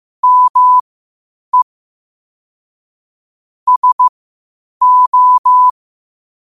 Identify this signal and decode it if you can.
Morse code, heard as a keyed tone, about 15 wpm, spaced as for 8 wpm, 1000 Hz. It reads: ME SO
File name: qotdfiles / 8wpm